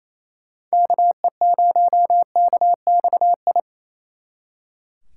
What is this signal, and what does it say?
Morse code, heard as a keyed tone, about 33 wpm, 700 Hz. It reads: KE0KXI
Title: Speed Racing - USA 2x3 Call Signs 28wpm